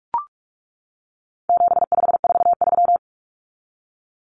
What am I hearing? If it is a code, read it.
Morse code, heard as a keyed tone, about 44 wpm, 700 Hz. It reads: 7543
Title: Common Cut Numbers - 4-Digit 45wpm